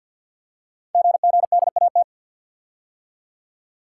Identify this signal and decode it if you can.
Morse code, heard as a keyed tone, about 49 wpm, 700 Hz. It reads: GGDAT
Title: Code Groups - 5 Letters - 50wpm